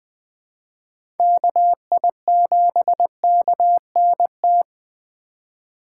Morse code, heard as a keyed tone, about 21 wpm, 700 Hz. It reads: KI7KNT
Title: USA Call Signs - 6 Characters - 3x Character Spacing - 20wpm